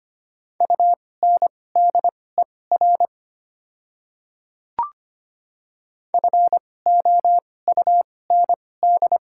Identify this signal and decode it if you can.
Morse code, heard as a keyed tone, about 25 wpm, spaced as for 18 wpm, 700 Hz. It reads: UNDER FOUND
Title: 2x Character Spacing - Top 300 Words - 25wpm